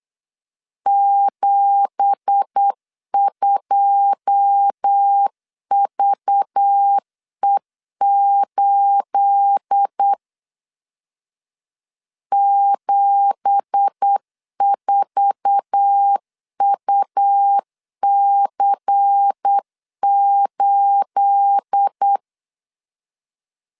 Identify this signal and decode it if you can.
Morse code, heard as a keyed tone, about 8 wpm, 800 Hz. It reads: 72VE8 74UC8
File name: LEZIONI / 9 2867 8wpm 3440